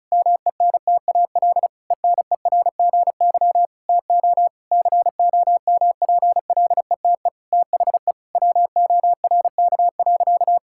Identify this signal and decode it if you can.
Morse code, heard as a keyed tone, about 36 wpm, 700 Hz. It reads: MENTAL ENERGY TO COMPLETE THE WORK.
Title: Sentences from Top 1000 Words - Review Part 6 - 35wpm